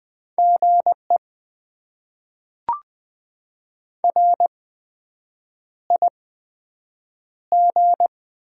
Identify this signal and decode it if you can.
Morse code, heard as a keyed tone, about 20 wpm, 700 Hz. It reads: G E R I G
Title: Top 20 QSO Elements - 8x Character Spacing - 20wpm